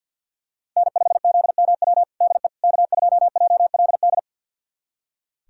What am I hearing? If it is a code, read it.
Morse code, heard as a keyed tone, about 46 wpm, 700 Hz. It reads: N5ZMW DE K2JLD
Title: USA Callsign DE Callsign - 5 Characters - 50wpm